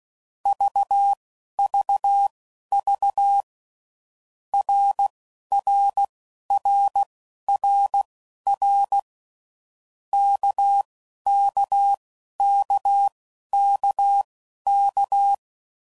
Morse code, heard as a keyed tone, about 16 wpm, spaced as for 11 wpm, 800 Hz. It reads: VVV RRRRR KKKKK
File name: Groups / Basic Letter Groups / RKPX